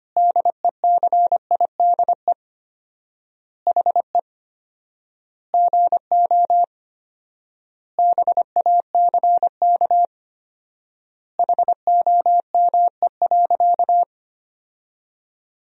Morse code, heard as a keyed tone, about 25 wpm, 700 Hz. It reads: DECIDE HE GO BACK HOME.
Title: 4x Word-Spacing - Sentences from Top 500 Words - 25wpm